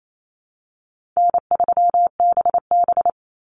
Morse code, heard as a keyed tone, about 28 wpm, 700 Hz. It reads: N3BB
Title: Easy Call Signs Repeated 3x - 28wpm